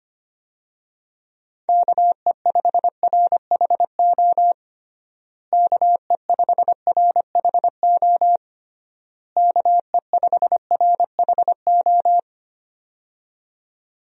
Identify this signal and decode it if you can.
Morse code, heard as a keyed tone, about 25 wpm, 700 Hz. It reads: KE5RHO KE5RHO KE5RHO
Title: USA Call Signs - 6 Characters - Repeated 3x - 25wpm